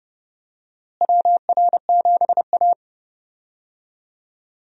Morse code, heard as a keyed tone, about 30 wpm, 700 Hz. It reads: WR7A
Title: USA Call Signs - 4 Characters - Repeated 2x - 30wpm